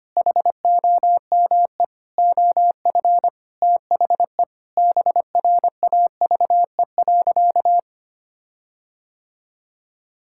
Morse code, heard as a keyed tone, about 25 wpm, 700 Hz. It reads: HOME OF THE BRAVE.
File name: practice / 25wpm / Sentences with 9-Word Phrases - 25wpm